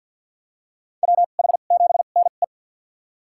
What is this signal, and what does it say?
Morse code, heard as a keyed tone, about 49 wpm, spaced as for 35 wpm, 700 Hz. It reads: WH7NE
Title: USA Call Signs - 5 Characters - 2x Character Spacing - 50wpm